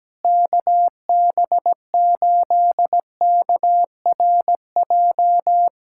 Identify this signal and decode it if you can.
Morse code, heard as a keyed tone, about 17 wpm, 700 Hz. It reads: KB8KRJ
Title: USA Callsign DE Callsign 17wpm